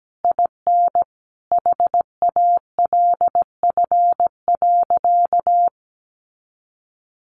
Morse code, heard as a keyed tone, about 17 wpm, 700 Hz. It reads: IN HALF.
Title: Sentences from Top 1100 Words 17wpm